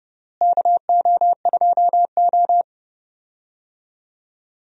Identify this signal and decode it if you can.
Morse code, heard as a keyed tone, about 32 wpm, 700 Hz. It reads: KO2O
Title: USA Call Signs - 4 Characters - Repeated 3x - 30wpm